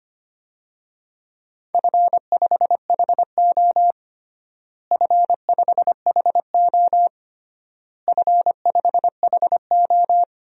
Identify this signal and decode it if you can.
Morse code, heard as a keyed tone, about 25 wpm, 700 Hz. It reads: F5HO F5HO F5HO
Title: DXCC-100 Call Signs - 4 Characters - Repeated 3x - 25wpm